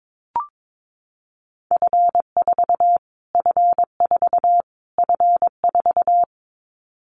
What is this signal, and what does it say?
Morse code, heard as a keyed tone, about 22 wpm, 700 Hz. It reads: F4 F4 F4
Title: Callsign Prefixes for Top 25 DXCC - Repeated 3x - 22wpm